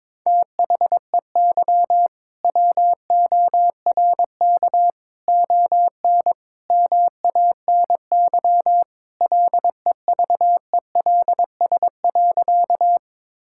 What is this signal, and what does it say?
Morse code, heard as a keyed tone, about 22 wpm, 700 Hz. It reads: THEY WORK ON MANY LEVELS.
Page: morse-code-ninja-practice-22wpm